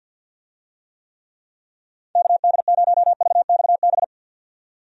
Morse code, heard as a keyed tone, about 49 wpm, 700 Hz. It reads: KD0VXB
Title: USA Call Signs - 6 Characters - 50wpm